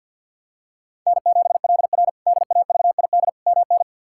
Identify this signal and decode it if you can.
Morse code, heard as a keyed tone, about 50 wpm, 700 Hz. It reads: N7LR DAVID MN
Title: K1USN SST QSO Practice - 50wpm